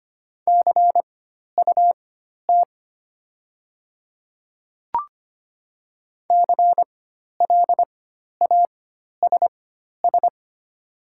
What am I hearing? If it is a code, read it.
Morse code, heard as a keyed tone, about 25 wpm, spaced as for 12 wpm, 690 Hz. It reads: CUT CLASS
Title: 4x Character Spacing - Top 500 Words - 25wpm